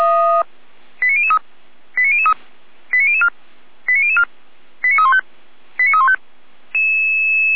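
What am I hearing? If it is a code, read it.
Five-tone selective call (ZVEI), heard as quick runs of five tones, tones of about 70 ms. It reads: 89003 89003 89004 89004 89315 89315
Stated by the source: Five-tone ZVEI selective call 89099, received by SDR.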